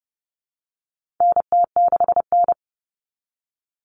Morse code, heard as a keyed tone, about 30 wpm, 700 Hz. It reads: NT6N
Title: All Cut Numbers - 4-Digit 30wpm